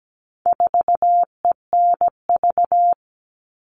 Morse code, heard as a keyed tone, about 17 wpm, 700 Hz. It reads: 4ENV
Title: All Cut Numbers - 4-Digit 17wpm